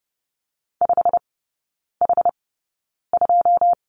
Morse code, heard as a keyed tone, about 30 wpm, spaced as for 9 wpm, 700 Hz. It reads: 5H2
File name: Lesson 19e - Letter w - Three Characters 30wpm